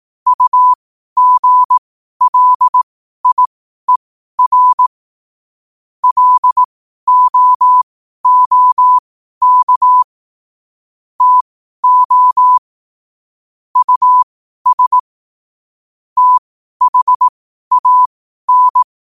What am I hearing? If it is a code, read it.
Morse code, heard as a keyed tone, about 18 wpm, spaced as for 12 wpm, 1000 Hz. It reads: UGLIER LOOK TO US THAN
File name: qotdfiles / 12wpm